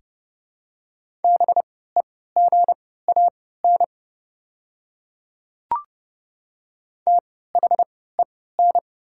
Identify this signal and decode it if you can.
Morse code, heard as a keyed tone, about 30 wpm, spaced as for 17 wpm, 700 Hz. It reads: BEGAN THEN
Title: 3x Character Spacing - Top 300 Words - 30wpm